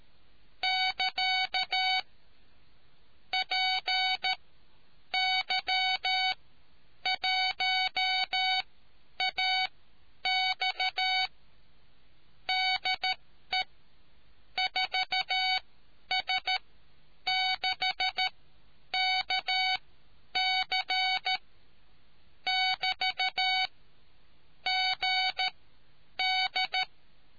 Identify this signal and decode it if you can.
Morse code, heard as a keyed tone, about 13 wpm, spaced as for 9 wpm, 750 Hz. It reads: <KA> PY1AX DE 4S6KC <BT> GD